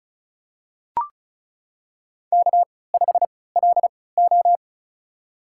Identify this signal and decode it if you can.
Morse code, heard as a keyed tone, about 34 wpm, spaced as for 20 wpm, 700 Hz. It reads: K5LO
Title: USA Call Signs - 4 Characters - 3x Character Spacing - 35wpm